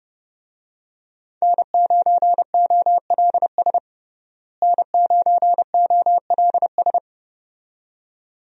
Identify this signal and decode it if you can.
Morse code, heard as a keyed tone, about 30 wpm, 700 Hz. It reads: N9OLS N9OLS
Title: USA Call Signs - 5 Characters - Repeated 2x - 30wpm